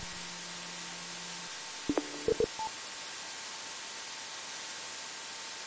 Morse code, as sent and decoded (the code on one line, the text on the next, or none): none